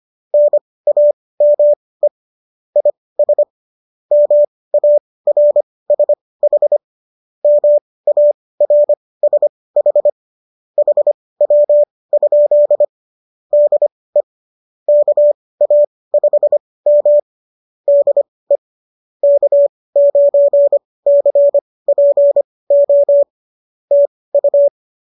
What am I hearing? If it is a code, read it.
Morse code, heard as a keyed tone, about 25 wpm, spaced as for 18 wpm, 590 Hz. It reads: NAME IS MARSH MARSH HW? DE KA5M DE K9CPO TU
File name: qso211 18